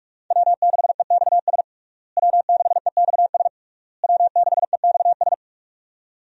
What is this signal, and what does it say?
Morse code, heard as a keyed tone, about 46 wpm, 700 Hz. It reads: W6EXS W6EXS W6EXS
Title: USA Call Signs - 5 Characters - Repeated 3x - 45wpm